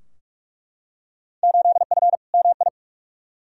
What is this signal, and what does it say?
Morse code, heard as a keyed tone, about 42 wpm, 700 Hz. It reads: 8F MI
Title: Field Day Exchanges Encore - 45wpm